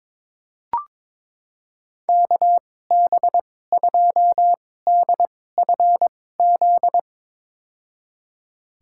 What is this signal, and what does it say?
Morse code, heard as a keyed tone, about 22 wpm, spaced as for 16 wpm, 700 Hz. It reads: KB2DFZ
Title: USA Call Signs - 6 Characters - 2x Character Spacing - 22wpm